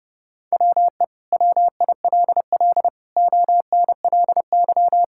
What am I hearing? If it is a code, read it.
Morse code, heard as a keyed tone, about 30 wpm, 700 Hz. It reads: WE WILL ONLY
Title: Sentences from Top 1000 Words - Review Part 5 - 30wpm